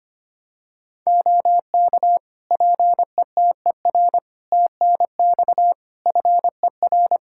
Morse code, heard as a keyed tone, about 25 wpm, 700 Hz. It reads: OK PETER TNX FER
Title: K1USN SST QSO Practice 25wpm